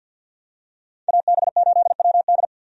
Morse code, heard as a keyed tone, about 52 wpm, 700 Hz. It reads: AB8WD
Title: USA Call Signs - 5 Characters - 50wpm